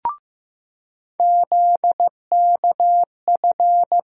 Morse code, heard as a keyed tone, about 15 wpm, 700 Hz. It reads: ZKF
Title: USA Call Sign Postfixes - 3 Letters 15wpm